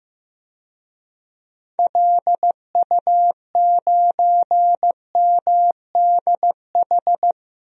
Morse code, heard as a keyed tone, about 15 wpm, 700 Hz. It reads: LU9MDH
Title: DXCC-100 Call Signs - 6 Characters - Repeated 2x - 15wpm